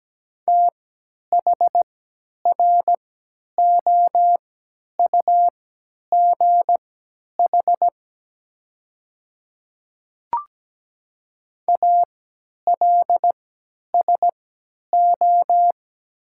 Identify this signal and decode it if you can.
Morse code, heard as a keyed tone, about 17 wpm, spaced as for 10 wpm, 700 Hz. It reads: THROUGH ALSO
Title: 3x Character Spacing - Top 500 Words - 17wpm